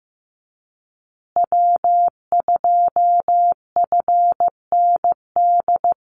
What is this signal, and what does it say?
Morse code, heard as a keyed tone, about 15 wpm, 700 Hz. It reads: W2FND
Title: USA Call Signs Repeated 2x 15wpm